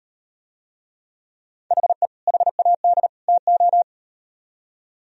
Morse code, heard as a keyed tone, about 38 wpm, 700 Hz. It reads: HE HAD TO